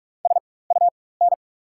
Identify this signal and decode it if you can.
Morse code, heard as a keyed tone, about 43 wpm, spaced as for 21 wpm, 690 Hz. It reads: SUN
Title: 4x Character Spacing - Top 300 Words - 45wpm